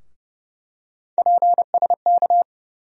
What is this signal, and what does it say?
Morse code, heard as a keyed tone, about 30 wpm, 700 Hz. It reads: PSK
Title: USA Call Sign Postfixes - 3 Letters 30wpm